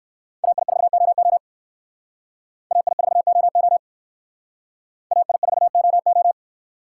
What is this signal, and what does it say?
Morse code, heard as a keyed tone, about 59 wpm, 700 Hz. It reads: AI4OO AI4OO AI4OO